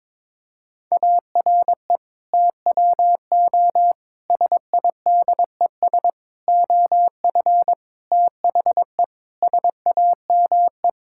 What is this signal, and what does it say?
Morse code, heard as a keyed tone, about 22 wpm, 700 Hz. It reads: ARE TWO SIDES OF THE SAME
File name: practice / 22wpm / Sets of 7 Words - 22wpm